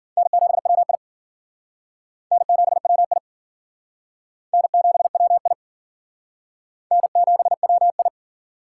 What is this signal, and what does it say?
Morse code, heard as a keyed tone, about 50 wpm, 700 Hz. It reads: N7WI N7WI N7WI N7WI
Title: Speed Racing - USA 1x2 Call Signs 40wpm